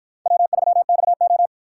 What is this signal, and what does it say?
Morse code, heard as a keyed tone, about 50 wpm, 700 Hz. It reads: W3XO